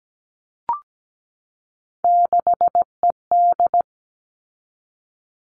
Morse code, heard as a keyed tone, about 17 wpm, 700 Hz. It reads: 6ED
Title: All Cut Numbers - 3-Digit 17wpm